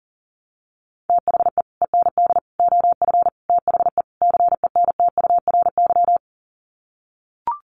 Morse code, heard as a keyed tone, about 40 wpm, 700 Hz. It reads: THE END OF THE CENTURY
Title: Sets of 5 Words Encore2 40wpm